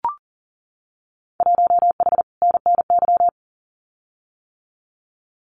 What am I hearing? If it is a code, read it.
Morse code, heard as a keyed tone, about 40 wpm, 700 Hz. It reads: 1H NNY
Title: Winter Field Day Exchanges 40wpm